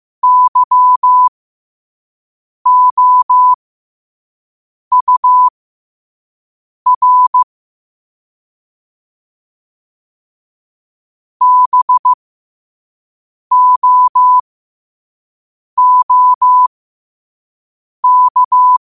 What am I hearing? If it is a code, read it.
Morse code, heard as a keyed tone, about 15 wpm, spaced as for 5 wpm, 1000 Hz. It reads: YOUR BOOK
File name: qotdfiles / 5wpm